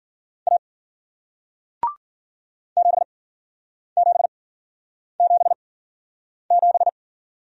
Morse code, heard as a keyed tone, about 49 wpm, spaced as for 9 wpm, 700 Hz. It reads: A 7777
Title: Speed-Racing Single Letter-Number 40wpm